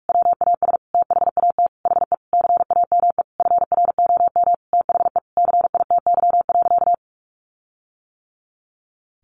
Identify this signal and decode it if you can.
Morse code, heard as a keyed tone, about 44 wpm, 700 Hz. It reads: WAS THAT HE CAME FROM THE CITY.
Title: Sentences from Top 1500 Words 45wpm